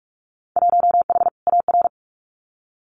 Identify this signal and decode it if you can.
Morse code, heard as a keyed tone, about 46 wpm, 700 Hz. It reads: 1H AR
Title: Winter Field Day Exchanges Encore 45wpm